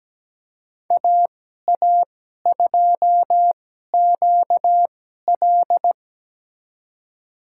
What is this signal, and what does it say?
Morse code, heard as a keyed tone, about 17 wpm, spaced as for 12 wpm, 700 Hz. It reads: AA2QL
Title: USA Call Signs - 5 Characters - 2x Character Spacing - 17wpm